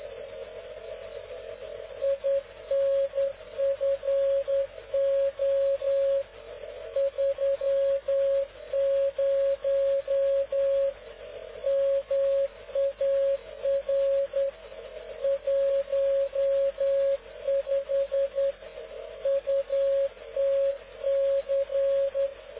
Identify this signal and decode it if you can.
Morse code, heard as a keyed tone, about 10 wpm, 550 Hz. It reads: INFO 30 MAR 15 UTC
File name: cw / ubacw dk0wcy slow